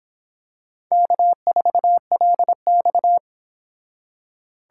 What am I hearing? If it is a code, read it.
Morse code, heard as a keyed tone, about 25 wpm, 700 Hz. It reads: K4LX